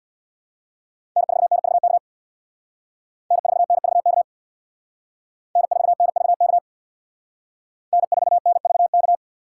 Morse code, heard as a keyed tone, about 62 wpm, 700 Hz. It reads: N4NVK N4NVK N4NVK N4NVK
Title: Speed Racing - USA 1x3 Call Signs 50wpm